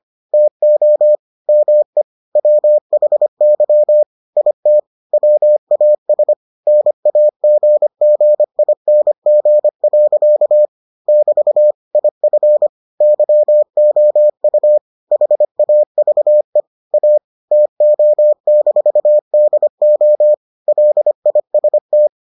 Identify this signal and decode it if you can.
Morse code, heard as a keyed tone, about 25 wpm, 600 Hz. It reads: TO ME WHY IT WAS NAGGING. <BT> IF YOU HAVE A TO-DO LIST